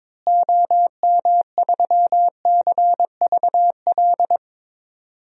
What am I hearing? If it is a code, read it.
Morse code, heard as a keyed tone, about 22 wpm, 700 Hz. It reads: OM3CVL
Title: DXCC-100 Call Signs - 6 Characters - Repeated 3x - 22wpm